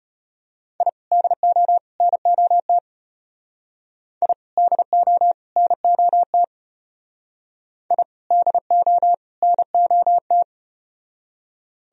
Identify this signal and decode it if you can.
Morse code, heard as a keyed tone, about 32 wpm, 700 Hz. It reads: I DO NOT I DO NOT I DO NOT